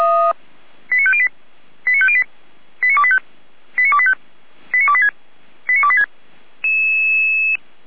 Five-tone selective call (ZVEI), heard as quick runs of five tones, tones of about 70 ms. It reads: 89508 89508 89285 89285 89286 89286